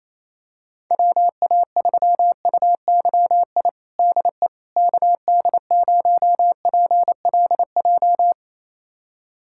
Morse code, heard as a keyed tone, about 27 wpm, 700 Hz. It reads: WA3UYI DE KD0PLJ
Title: USA Callsign DE Callsign - 6 Characters - 28wpm